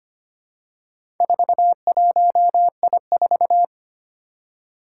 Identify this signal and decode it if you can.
Morse code, heard as a keyed tone, about 25 wpm, 700 Hz. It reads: 41I4
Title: Four Challenging Letter-Number Combinations 25wpm